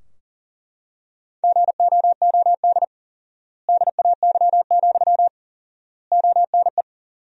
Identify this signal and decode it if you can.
Morse code, heard as a keyed tone, about 40 wpm, 700 Hz. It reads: GOOD DAY, ONE
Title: 4x Word-Spacing - Sentences from Top 200 Words - 40wpm